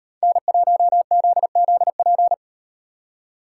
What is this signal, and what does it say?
Morse code, heard as a keyed tone, about 38 wpm, 700 Hz. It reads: N1ZZP